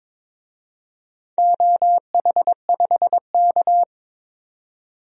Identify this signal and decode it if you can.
Morse code, heard as a keyed tone, about 21 wpm, 700 Hz. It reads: OH5K
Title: DXCC-100 Call Signs - 4 Characters - 22wpm